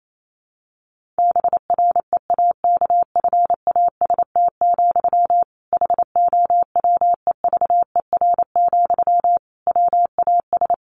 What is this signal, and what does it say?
Morse code, heard as a keyed tone, about 28 wpm, 700 Hz. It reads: BREAKFAST, HOWEVER, WAS
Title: Sentences from Top 1500 Words 28wpm